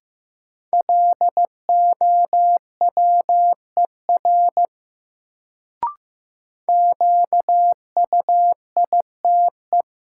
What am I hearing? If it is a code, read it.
Morse code, heard as a keyed tone, about 15 wpm, 700 Hz. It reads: LOWER QUITE
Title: Sentences from Top 700 Words - New Word Drill - 15wpm